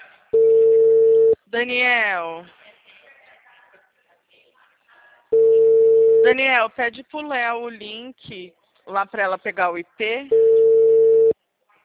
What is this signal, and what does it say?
Ringback tone (China, 450 Hz), with speech.
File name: out- -2019- - - .1327